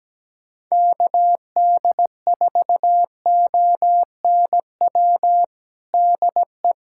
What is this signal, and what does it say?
Morse code, heard as a keyed tone, about 17 wpm, 700 Hz. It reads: KD4ONW DE
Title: USA Callsign DE Callsign - 6 Characters - 17wpm